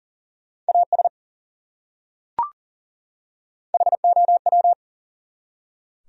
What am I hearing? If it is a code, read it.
Morse code, heard as a keyed tone, about 40 wpm, 700 Hz. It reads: AS HOW
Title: Top 100 Words 40wpm